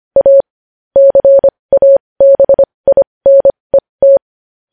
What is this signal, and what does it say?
Morse code, heard as a keyed tone, about 25 wpm, spaced as for 20 wpm, 550 Hz. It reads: A CABINET
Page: news-headlines-in-morse-code-at-20-wpm